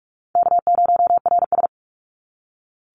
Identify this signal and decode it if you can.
Morse code, heard as a keyed tone, about 46 wpm, 700 Hz. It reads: K0RS
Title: NA SOTA Chaser Callsigns 45wpm